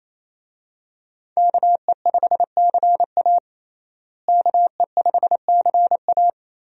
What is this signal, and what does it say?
Morse code, heard as a keyed tone, about 28 wpm, 700 Hz. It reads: KE5CA KE5CA
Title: USA Call Signs - 5 Characters - Repeated 2x - 28wpm